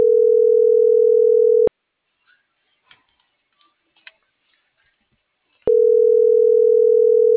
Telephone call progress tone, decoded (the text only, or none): ringback tone